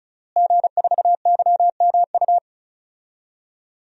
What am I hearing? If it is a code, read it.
Morse code, heard as a keyed tone, about 34 wpm, 700 Hz. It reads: G4YMU
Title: DXCC-100 Call Signs - 5 Characters - Repeated 2x - 35wpm